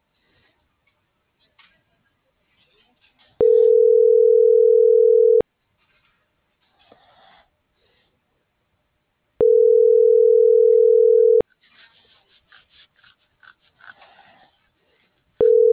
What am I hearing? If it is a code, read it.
Ringback tone (North America), 440 + 480 Hz.